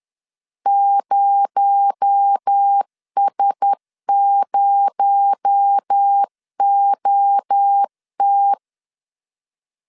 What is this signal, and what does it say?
Morse code, heard as a keyed tone, about 10 wpm, 800 Hz. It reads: 0S0OT
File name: LEZIONI / 6 PKX0 / 6 PKX0 10wpm 2707